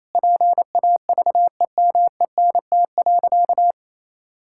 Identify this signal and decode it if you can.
Morse code, heard as a keyed tone, about 28 wpm, 700 Hz. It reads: PAVEMENT.
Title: Pangram Challenge v2 - 28wpm